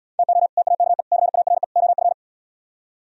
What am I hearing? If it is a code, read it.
Morse code, heard as a keyed tone, about 77 wpm, 700 Hz. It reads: TO TAKE CARE OF